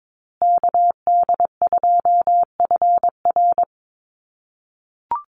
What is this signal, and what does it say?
Morse code, heard as a keyed tone, about 22 wpm, 700 Hz. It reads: KD2FR
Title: USA Call Signs Repeated 2x 22wpm